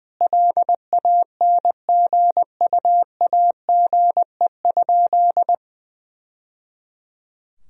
Morse code, heard as a keyed tone, about 21 wpm, 700 Hz. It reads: LANGUAGE?
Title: Sentences from Top 1000 Words - Review Part 3 - 20wpm